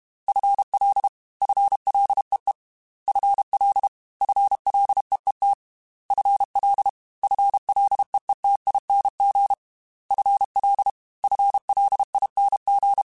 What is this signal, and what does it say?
Morse code, heard as a keyed tone, about 32 wpm, 800 Hz. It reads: FL FLEE FL FLEET FL FLEETING FL FLING
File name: FL / FL mp3s / FL Prefixes 1x1 @30